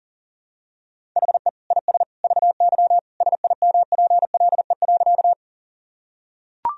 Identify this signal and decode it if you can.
Morse code, heard as a keyed tone, about 40 wpm, 700 Hz. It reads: HE IS VY SIMPLE.
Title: Sentences from Top 400 Words and Abbreviations 40wpm